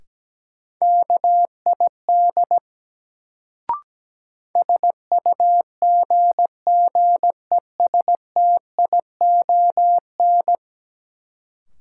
Morse code, heard as a keyed tone, about 17 wpm, 700 Hz. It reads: KID SUGGESTION
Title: Sentences from Top 1600 Words - New Word Drill - 17wpm